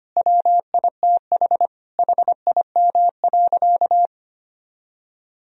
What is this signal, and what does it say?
Morse code, heard as a keyed tone, about 25 wpm, 700 Hz. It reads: WITH HIM.
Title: Sentences from Top 1000 Words - Review Part 4 - 25wpm